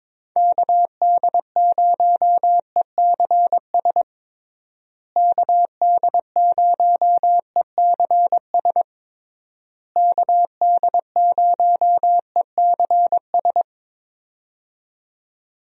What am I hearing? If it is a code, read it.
Morse code, heard as a keyed tone, about 22 wpm, 700 Hz. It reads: KD0ECS KD0ECS KD0ECS
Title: USA Call Signs - 6 Characters - Repeated 3x - 22wpm